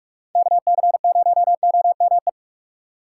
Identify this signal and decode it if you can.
Morse code, heard as a keyed tone, about 46 wpm, 700 Hz. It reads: KC0OME
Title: USA Call Signs - 6 Characters - 2x Character Spacing - 45wpm